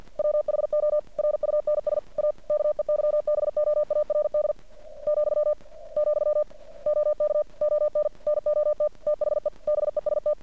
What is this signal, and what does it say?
Morse code, heard as a keyed tone, about 50 wpm, 600 Hz. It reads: WHO RUNS A KEYBOARD , , OK ON NOT THE BEST
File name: archived / CRSnet